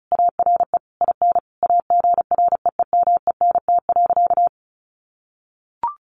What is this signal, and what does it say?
Morse code, heard as a keyed tone, about 34 wpm, 700 Hz. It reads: ARE IN AGREEMENT.
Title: Sentences from Top 1500 Words 35wpm